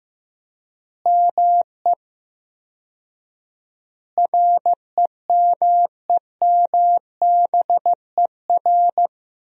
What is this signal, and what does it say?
Morse code, heard as a keyed tone, about 15 wpm, 700 Hz. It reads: ME REMEMBER
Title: 4x Word-Spacing - Sentences from Top 400 Words - 15wpm